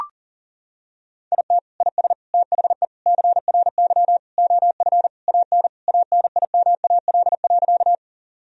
Morse code, heard as a keyed tone, about 40 wpm, 700 Hz. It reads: IT IS THE CRY OF AN ANIMAL.
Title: Sentences from Top 1000 Words - Review Part 2 - 40wpm